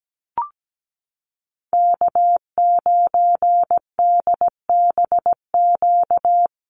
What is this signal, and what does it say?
Morse code, heard as a keyed tone, about 17 wpm, 700 Hz. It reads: K9DBQ
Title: USA Call Signs Repeated 3x 17wpm